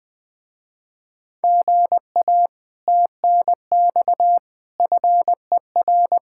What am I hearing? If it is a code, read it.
Morse code, heard as a keyed tone, about 20 wpm, 700 Hz. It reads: GA TNX FER
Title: Speed-Racing Common QSO Phrases - 17wpm